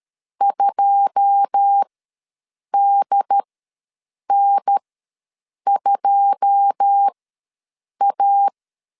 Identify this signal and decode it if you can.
Morse code, heard as a keyed tone, about 13 wpm, spaced as for 7 wpm, 800 Hz. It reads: 2DN2A
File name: LEZIONI / 9 2 ws 12wpm 3406